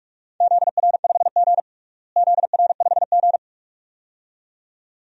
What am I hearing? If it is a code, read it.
Morse code, heard as a keyed tone, about 46 wpm, 700 Hz. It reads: ZR5G ZR5G
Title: DXCC-100 Call Signs - 4 Characters - Repeated 2x - 45wpm